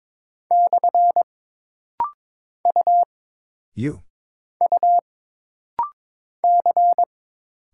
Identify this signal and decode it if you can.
Morse code, heard as a keyed tone, about 22 wpm, 700 Hz. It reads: / U U C
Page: This rapid-fire timing is geared to push your ICR (Instant Character Recognition) skills.